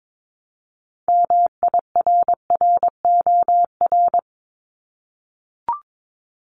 Morse code, heard as a keyed tone, about 22 wpm, 700 Hz. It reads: MIRROR
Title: Multi-letter Phonemes 22wpm